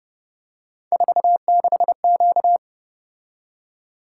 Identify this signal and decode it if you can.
Morse code, heard as a keyed tone, about 29 wpm, 700 Hz. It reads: 46Q
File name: Speed-Racing - Three Letter-Number Combinations - 25wpm